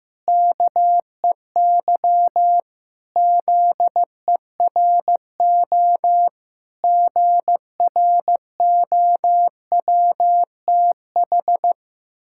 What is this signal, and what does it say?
Morse code, heard as a keyed tone, about 16 wpm, 700 Hz. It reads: KEY ZERO GROWTH
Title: Pangram Challenge v2 - 15wpm